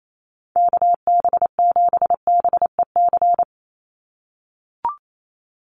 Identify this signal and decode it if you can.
Morse code, heard as a keyed tone, about 28 wpm, 700 Hz. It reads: KB7BEC
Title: USA Call Signs 28wpm